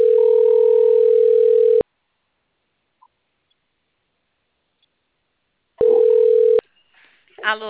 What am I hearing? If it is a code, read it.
Ringback tone (North America), 440 + 480 Hz.